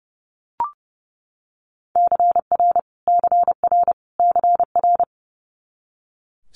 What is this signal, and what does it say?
Morse code, heard as a keyed tone, about 30 wpm, 700 Hz. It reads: CR CR CR
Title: Multi-letter Phonemes Repeated 3x 30wpm